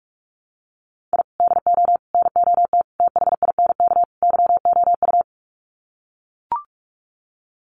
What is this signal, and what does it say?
Morse code, heard as a keyed tone, about 44 wpm, 700 Hz. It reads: I DO NOT THINK YOU
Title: Sets of 5 Words 45wpm